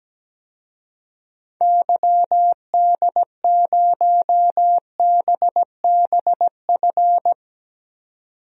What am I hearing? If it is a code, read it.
Morse code, heard as a keyed tone, about 17 wpm, 700 Hz. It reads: YD0BBF
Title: DXCC-100 Call Signs - 6 Characters - 17wpm